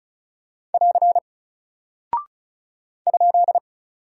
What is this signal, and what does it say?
Morse code, heard as a keyed tone, about 36 wpm, 700 Hz. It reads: <AR> ?
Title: Single Character (Letter:Number:Punctuation:Prosign) Alt 35wpm